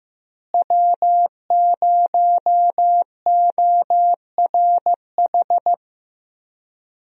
Morse code, heard as a keyed tone, about 15 wpm, 700 Hz. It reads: W0ORH
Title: USA Call Signs - 5 Characters - Repeated 3x - 15wpm